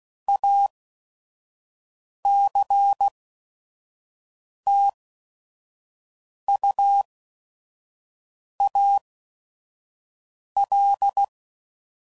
Morse code, heard as a keyed tone, about 16 wpm, spaced as for 5 wpm, 800 Hz. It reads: ACTUAL
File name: CW Code Groups @5WPM / Technical Subjects / XL